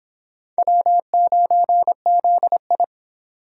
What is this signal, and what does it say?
Morse code, heard as a keyed tone, about 25 wpm, 700 Hz. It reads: W9ZI